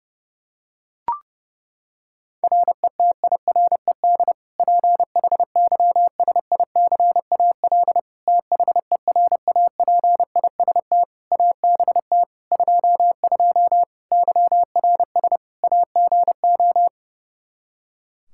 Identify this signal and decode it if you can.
Morse code, heard as a keyed tone, about 30 wpm, 700 Hz. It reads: RETIRED PHYSICAL THERAPIST ABT 22 YRS AGO
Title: Practice QSO - Exchange 3 - More Info - 30wpm